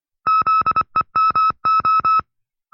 Morse code, heard as a keyed tone, about 24 wpm, 1310 Hz. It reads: ZEMO
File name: FreeFiles / CW-practice / 20WPM-side2